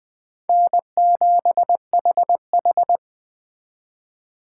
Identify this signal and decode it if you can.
Morse code, heard as a keyed tone, about 20 wpm, 700 Hz. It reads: N7HH